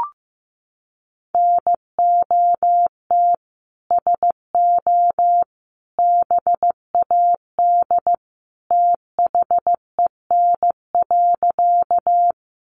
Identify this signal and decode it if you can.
Morse code, heard as a keyed tone, about 15 wpm, 700 Hz. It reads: NOT SO BAD THEN.
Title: Sentences from Top 700 Words 15wpm